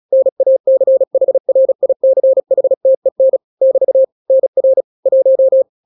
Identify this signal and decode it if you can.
Morse code, heard as a keyed tone, about 36 wpm, 530 Hz. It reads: NACHRICHTEN <BT> NR 1
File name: daily / 35wpm